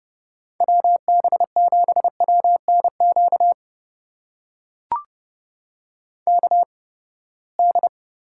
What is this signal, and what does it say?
Morse code, heard as a keyed tone, about 30 wpm, spaced as for 21 wpm, 700 Hz. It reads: WB7WNQ KD
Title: USA Call Signs - 6 Characters - 8x Character Spacing - 30wpm